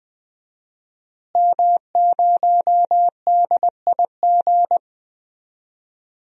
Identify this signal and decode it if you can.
Morse code, heard as a keyed tone, about 20 wpm, 700 Hz. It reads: M0DIG
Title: DXCC-100 Call Signs - 5 Characters - Repeated 3x - 20wpm